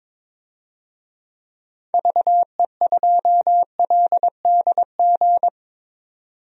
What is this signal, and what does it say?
Morse code, heard as a keyed tone, about 22 wpm, 700 Hz. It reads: VE2LDG
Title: DXCC-100 Call Signs - 6 Characters - 22wpm